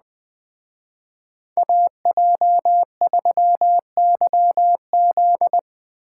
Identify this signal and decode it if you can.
Morse code, heard as a keyed tone, about 21 wpm, 700 Hz. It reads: AJ3YZ
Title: Speed Racing - USA 2x2 Call Signs 17wpm